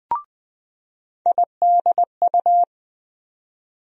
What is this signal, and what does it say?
Morse code, heard as a keyed tone, about 20 wpm, 700 Hz. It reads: IDU
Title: USA Call Sign Postfixes - 3 Letters 20wpm